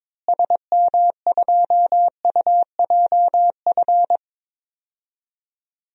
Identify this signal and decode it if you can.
Morse code, heard as a keyed tone, about 22 wpm, 700 Hz. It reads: SM2UJF
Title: DXCC-100 Call Signs - 6 Characters - Repeated 2x - 22wpm